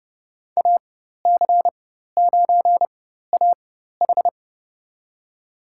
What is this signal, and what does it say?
Morse code, heard as a keyed tone, about 32 wpm, spaced as for 14 wpm, 700 Hz. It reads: AC9AH
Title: USA Call Signs - 5 Characters - 4x Character Spacing - 30wpm